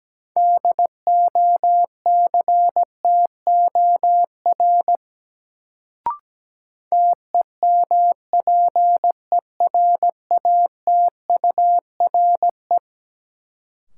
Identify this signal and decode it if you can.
Morse code, heard as a keyed tone, about 17 wpm, 700 Hz. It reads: DOCTOR TEMPERATURE
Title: Sentences from Top 1500 Words - New Word Drill Alt - 17wpm